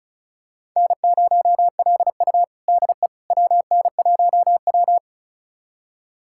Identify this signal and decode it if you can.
Morse code, heard as a keyed tone, about 36 wpm, 700 Hz. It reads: N0LU DE WN1W
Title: USA Callsign DE Callsign - 4 Characters - 35wpm